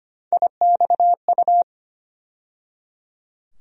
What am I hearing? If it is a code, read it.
Morse code, heard as a keyed tone, about 25 wpm, 700 Hz. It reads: IXU
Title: USA Call Sign Postfixes - 3 Letters 25wpm